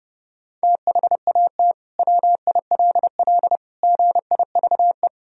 Morse code, heard as a keyed tone, about 32 wpm, 700 Hz. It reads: THAT WILL GIVE
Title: Sentences from Top 1000 Words - Review Part 1 - 30wpm